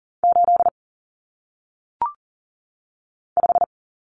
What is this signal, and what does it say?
Morse code, heard as a keyed tone, about 39 wpm, 700 Hz. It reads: 8 5
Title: Numbers - 1-Digit 40wpm